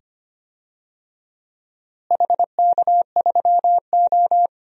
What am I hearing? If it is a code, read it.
Morse code, heard as a keyed tone, about 25 wpm, 700 Hz. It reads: HK3O
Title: DXCC-100 Call Signs - 4 Characters - Repeated 2x - 25wpm